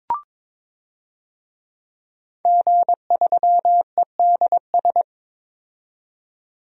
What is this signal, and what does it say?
Morse code, heard as a keyed tone, about 22 wpm, 700 Hz. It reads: G3EDS
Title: DXCC-100 Call Signs - 5 Characters - 22wpm